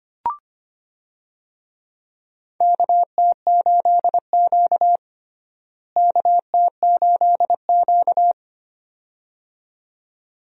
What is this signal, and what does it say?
Morse code, heard as a keyed tone, about 25 wpm, 700 Hz. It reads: KT8Q KT8Q
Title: USA Call Signs - 4 Characters - Repeated 2x - 25wpm